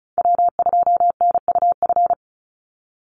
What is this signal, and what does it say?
Morse code, heard as a keyed tone, about 34 wpm, 700 Hz. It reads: W2NUF
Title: USA Call Signs 35wpm